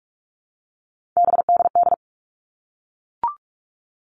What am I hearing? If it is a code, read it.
Morse code, heard as a keyed tone, about 44 wpm, 700 Hz. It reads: BDD
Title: All Cut Numbers - 3-Digit 45wpm